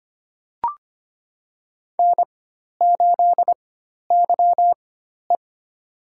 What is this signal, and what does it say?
Morse code, heard as a keyed tone, about 25 wpm, spaced as for 12 wpm, 700 Hz. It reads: N8YE
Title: USA Call Signs - 4 Characters - 4x Character Spacing - 25wpm